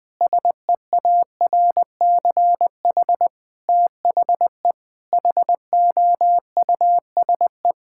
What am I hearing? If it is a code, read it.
Morse code, heard as a keyed tone, about 20 wpm, 700 Hz. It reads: SEARCH THE HOUSE
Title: Sentences from Top 1000 Words - Review Part 5 - 20wpm